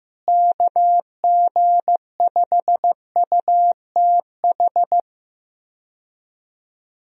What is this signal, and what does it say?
Morse code, heard as a keyed tone, about 15 wpm, 700 Hz. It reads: KG5UTH
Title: USA Call Signs - 6 Characters - Repeated 3x - 15wpm